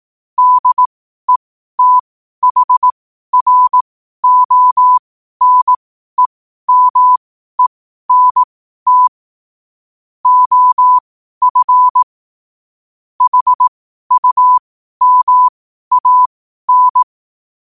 Morse code, heard as a keyed tone, about 18 wpm, spaced as for 12 wpm, 1000 Hz. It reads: DETHRONEMENT OF HUMAN
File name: qotdfiles / 12wpm